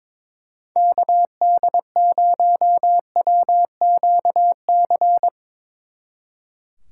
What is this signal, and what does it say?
Morse code, heard as a keyed tone, about 22 wpm, 700 Hz. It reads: KD0WQC
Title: Speed Racing - USA 2x3 Call Signs 22wpm